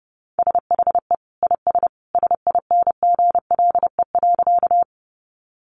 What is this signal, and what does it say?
Morse code, heard as a keyed tone, about 30 wpm, 700 Hz. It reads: SHE IS SINGLE.
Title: Sentences from Top 700 Words 30wpm